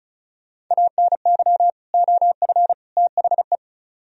Morse code, heard as a keyed tone, about 43 wpm, 700 Hz. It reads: ANY OF THE
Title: Speed-Racing - Sets of 3 Words 35wpm